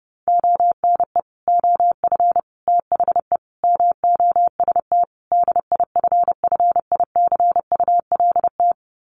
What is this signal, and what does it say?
Morse code, heard as a keyed tone, about 30 wpm, 700 Hz. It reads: ONE OF THE MOST DIFFICULT
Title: Sets of 5 Words Encore2 30wpm